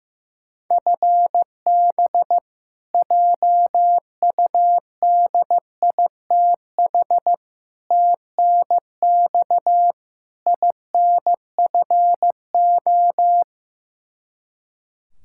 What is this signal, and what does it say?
Morse code, heard as a keyed tone, about 15 wpm, 700 Hz. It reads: FB JUDITH TNX INFO
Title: K1USN SST QSO Practice 15wpm